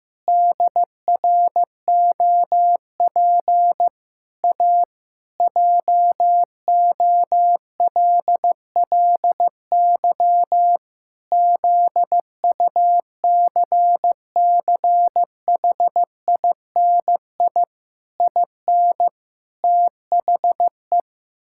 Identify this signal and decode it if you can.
Morse code, heard as a keyed tone, about 15 wpm, 700 Hz. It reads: DROP A JOLLY ZUCCHINI IN THE
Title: Pangram Challenge v2 - 15wpm